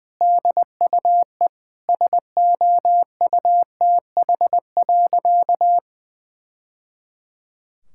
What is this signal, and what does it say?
Morse code, heard as a keyed tone, about 20 wpm, 700 Hz. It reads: DUE SOUTH.
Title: Sentences from Top 1000 Words - Review Part 6 - 20wpm